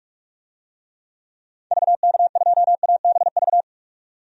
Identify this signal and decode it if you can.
Morse code, heard as a keyed tone, about 41 wpm, 700 Hz. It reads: VK2ABV
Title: DXCC-100 Call Signs - 6 Characters - 45wpm